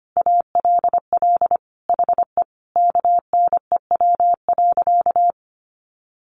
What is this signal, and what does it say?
Morse code, heard as a keyed tone, about 25 wpm, 700 Hz. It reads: ALL HE KNEW.
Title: Sentences from Top 400 Words 25wpm